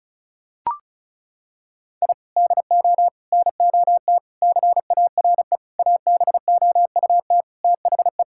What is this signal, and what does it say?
Morse code, heard as a keyed tone, about 36 wpm, 700 Hz. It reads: I DO NOT CARE ABOUT THE
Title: Sentences from Top 1000 Words - Review Part 5 - 35wpm